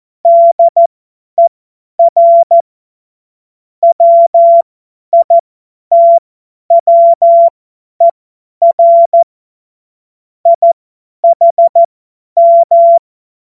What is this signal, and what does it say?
Morse code, heard as a keyed tone, about 14 wpm, spaced as for 10 wpm, 680 Hz. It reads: DER WITWER IHM
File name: Morsekurs / Witze / Witz 08. Februar 2022 50 F70